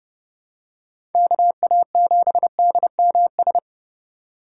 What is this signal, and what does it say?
Morse code, heard as a keyed tone, about 32 wpm, 700 Hz. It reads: KA7DMS
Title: USA Call Signs - 6 Characters - 2x Character Spacing - 30wpm